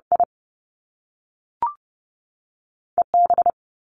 Morse code, heard as a keyed tone, about 30 wpm, spaced as for 3 wpm, 700 Hz. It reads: IE B
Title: Top Two Letter Combinations in Words 30wpm